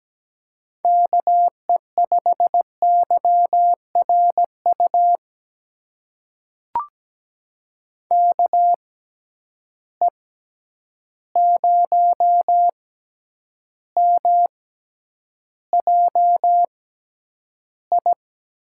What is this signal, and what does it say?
Morse code, heard as a keyed tone, about 17 wpm, spaced as for 9 wpm, 700 Hz. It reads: KE5YRU KE0MJI
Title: USA Call Signs - 6 Characters - 6x Character Spacing - 17wpm